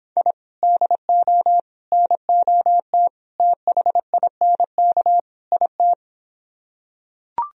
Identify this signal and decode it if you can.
Morse code, heard as a keyed tone, about 26 wpm, 700 Hz. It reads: I DO NOT THINK IT